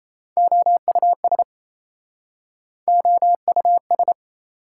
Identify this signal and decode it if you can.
Morse code, heard as a keyed tone, about 37 wpm, 690 Hz. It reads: OUS OUS